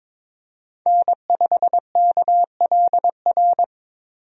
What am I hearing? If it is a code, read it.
Morse code, heard as a keyed tone, about 22 wpm, 700 Hz. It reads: N5KLR
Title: USA Call Signs - 5 Characters - Repeated 3x - 22wpm